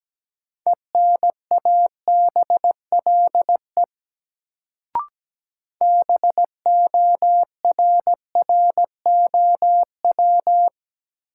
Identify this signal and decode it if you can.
Morse code, heard as a keyed tone, about 17 wpm, 700 Hz. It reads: ENABLE BORROW
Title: Sentences from Top 1500 Words - New Word Drill - 17wpm